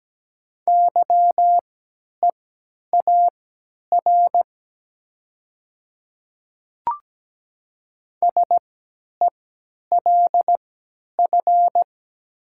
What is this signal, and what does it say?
Morse code, heard as a keyed tone, about 17 wpm, spaced as for 10 wpm, 700 Hz. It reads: YEAR SELF
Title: 3x Character Spacing - Top 300 Words - 17wpm